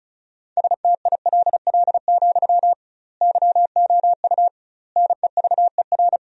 Morse code, heard as a keyed tone, about 36 wpm, 700 Hz. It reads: STILL, YOU NEVER
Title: 2x Word-Spacing - Sentences from Top 300 Words - 35wpm